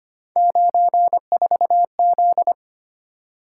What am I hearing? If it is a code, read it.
Morse code, heard as a keyed tone, about 25 wpm, 700 Hz. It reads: 94Z
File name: Three Letter-Number Combinations - 25wpm